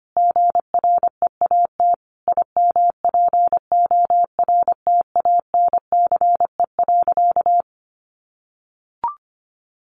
Morse code, heard as a keyed tone, about 25 wpm, 700 Hz. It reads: GREAT IMPORTANCE.
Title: Sentences from Top 1200 Words 25wpm